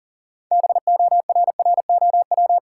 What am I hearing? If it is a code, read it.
Morse code, heard as a keyed tone, about 40 wpm, 700 Hz. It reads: BORROW
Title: Sentences from Top 1500 Words - New Word Drill Alt - 40wpm